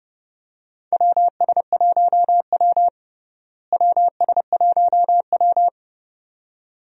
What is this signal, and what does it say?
Morse code, heard as a keyed tone, about 32 wpm, 700 Hz. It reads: WS1W WS1W
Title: USA Call Signs - 4 Characters - Repeated 2x - 30wpm